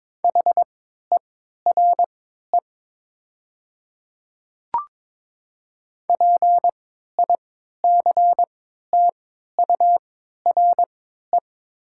Morse code, heard as a keyed tone, about 22 wpm, spaced as for 12 wpm, 700 Hz. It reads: HERE PICTURE
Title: 3x Character Spacing - Top 200 Words - 22wpm